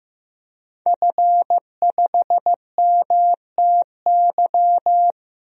Morse code, heard as a keyed tone, about 15 wpm, 700 Hz. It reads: F5MTY
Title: DXCC-100 Call Signs - 5 Characters - Repeated 3x - 15wpm